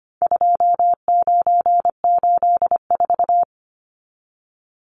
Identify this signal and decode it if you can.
Morse code, heard as a keyed tone, about 25 wpm, 700 Hz. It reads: 2984
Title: Numbers - 4-Digit 25wpm